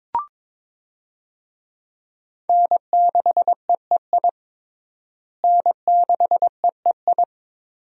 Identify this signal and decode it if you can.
Morse code, heard as a keyed tone, about 22 wpm, 700 Hz. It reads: N6EEI N6EEI
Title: USA Call Signs - 5 Characters - Repeated 2x - 22wpm